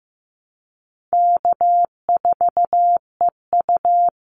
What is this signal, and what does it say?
Morse code, heard as a keyed tone, about 15 wpm, 700 Hz. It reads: K4EU
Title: Call Signs Repeated 3x - 15wpm